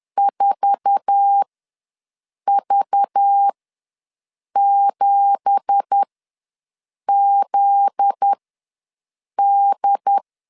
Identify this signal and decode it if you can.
Morse code, heard as a keyed tone, about 10 wpm, spaced as for 6 wpm, 800 Hz. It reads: 4V7ZD
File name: LEZIONI / 9 2867 ws 10wpm 4215